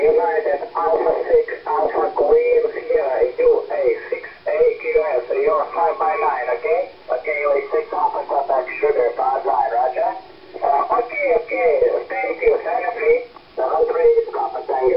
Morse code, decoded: none